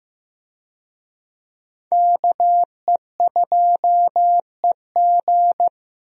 Morse code, heard as a keyed tone, about 15 wpm, 700 Hz. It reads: KE2EG
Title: USA Call Signs - 5 Characters - Repeated 2x - 15wpm